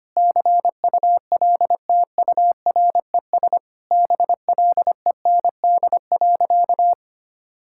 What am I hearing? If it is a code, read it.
Morse code, heard as a keyed tone, about 25 wpm, 700 Hz. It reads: CULTURES BLEND.
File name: practice / 25wpm / Sentences with 4-Word Phrases - 25wpm